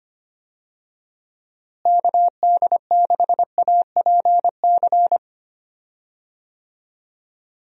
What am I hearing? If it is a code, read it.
Morse code, heard as a keyed tone, about 25 wpm, 700 Hz. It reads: KD6APC
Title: USA Call Signs - 6 Characters - Repeated 2x - 25wpm